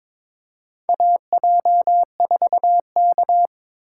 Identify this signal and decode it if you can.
Morse code, heard as a keyed tone, about 22 wpm, 700 Hz. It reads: AJ4K
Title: USA Call Signs - 4 Characters - 8x Character Spacing - 22wpm